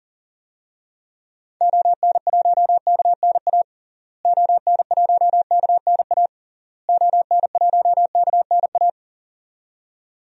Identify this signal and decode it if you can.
Morse code, heard as a keyed tone, about 37 wpm, 700 Hz. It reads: ON1KNA ON1KNA ON1KNA
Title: DXCC-100 Call Signs - 6 Characters - Repeated 3x - 40wpm